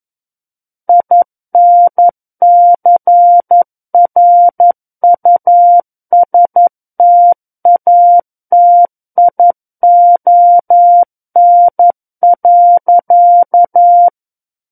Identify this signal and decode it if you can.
Morse code, heard as a keyed tone, about 11 wpm, 700 Hz. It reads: INCRUSTATION.
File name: Book One / 02-Chapter 2 - 11 WPM